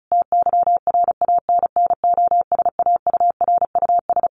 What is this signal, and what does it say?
Morse code, heard as a keyed tone, about 34 wpm, 700 Hz. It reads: TYRANNOSAURUS
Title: A Sound Of Thunder 35wpm